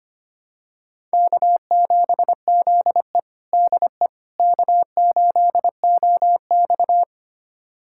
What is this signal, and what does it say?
Morse code, heard as a keyed tone, about 25 wpm, 700 Hz. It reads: K7ZE DE K8OX
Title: USA Callsign DE Callsign - 4 Characters - 25wpm